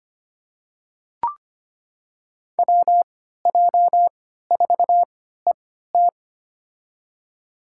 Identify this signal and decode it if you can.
Morse code, heard as a keyed tone, about 25 wpm, spaced as for 14 wpm, 700 Hz. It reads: WJ4ET
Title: USA Call Signs - 3x Character Spacing - 25wpm